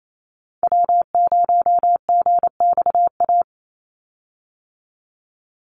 Morse code, heard as a keyed tone, about 28 wpm, 700 Hz. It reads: W0GXA
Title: Call Signs Repeated 2x - 28wpm